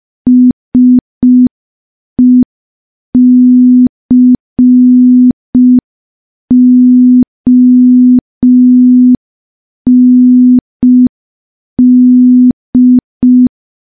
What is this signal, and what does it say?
Morse code, heard as a keyed tone, about 5 wpm, 250 Hz. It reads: SECOND
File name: morse / morse-training-05WPM